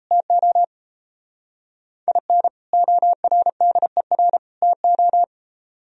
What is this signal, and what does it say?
Morse code, heard as a keyed tone, about 35 wpm, 700 Hz. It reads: TO IN ORDER TO
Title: Speed-Racing - Sets of 3 Words 28wpm